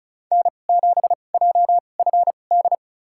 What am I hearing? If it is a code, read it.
Morse code, heard as a keyed tone, about 35 wpm, spaced as for 25 wpm, 700 Hz. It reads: N7JFD
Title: USA Call Signs - 5 Characters - 2x Character Spacing - 35wpm